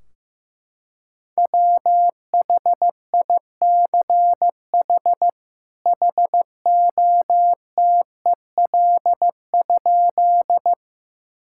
Morse code, heard as a keyed tone, about 15 wpm, 700 Hz. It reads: WHICH HOTEL?
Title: Sentences from Top 1600 Words 15wpm